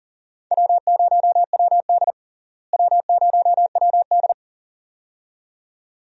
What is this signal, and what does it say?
Morse code, heard as a keyed tone, about 40 wpm, 700 Hz. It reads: W0WD W0WD
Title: USA Call Signs - 4 Characters - Repeated 2x - 40wpm